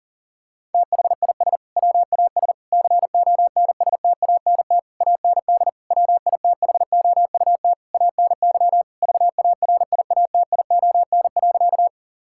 Morse code, heard as a keyed tone, about 40 wpm, 700 Hz. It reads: THIS WAS CONSTANT AND WITHOUT ANY VARIATION.
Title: Sentences from Top 1600 Words 40wpm